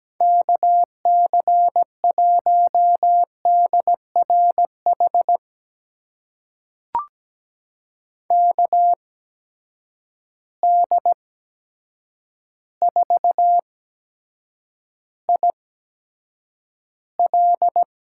Morse code, heard as a keyed tone, about 17 wpm, spaced as for 8 wpm, 700 Hz. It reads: KC1DRH KD4IL
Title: USA Call Signs - 8x Character Spacing - 17wpm